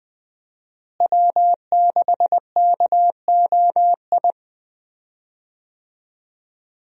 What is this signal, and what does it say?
Morse code, heard as a keyed tone, about 20 wpm, 700 Hz. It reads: W6KOI
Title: USA Call Signs - 5 Characters - Repeated 3x - 20wpm